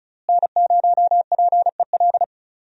Morse code, heard as a keyed tone, about 34 wpm, 700 Hz. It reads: N0PEL
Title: USA Call Signs - 5 Characters - Repeated 3x - 35wpm